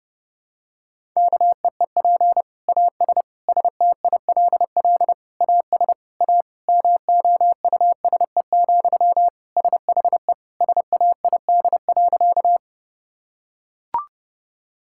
Morse code, heard as a keyed tone, about 30 wpm, 700 Hz. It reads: KEEP AS STILL AS A MOUSE, SHE SAID.
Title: Sentences from Top 1000 Words - Review Part 5 - 30wpm